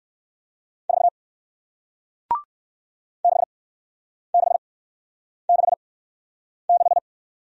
Morse code, heard as a keyed tone, about 59 wpm, spaced as for 9 wpm, 700 Hz. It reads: 4 6666